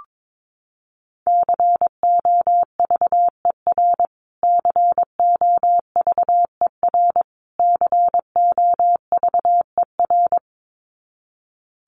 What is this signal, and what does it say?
Morse code, heard as a keyed tone, about 22 wpm, 700 Hz. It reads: COVER COVER COVER
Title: Top 500 Words Repeated 22wpm